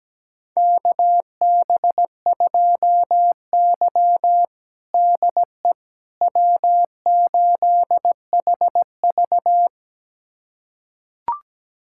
Morse code, heard as a keyed tone, about 17 wpm, 700 Hz. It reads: KB2Y DE W8HV
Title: USA Callsign DE Callsign - 4 Characters - 17wpm